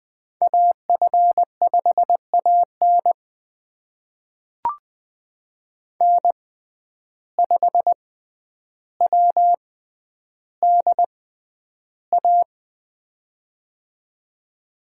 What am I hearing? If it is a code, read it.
Morse code, heard as a keyed tone, about 20 wpm, spaced as for 10 wpm, 700 Hz. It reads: AF5AN N5WDA
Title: USA Call Signs - 5 Characters - 6x Character Spacing - 20wpm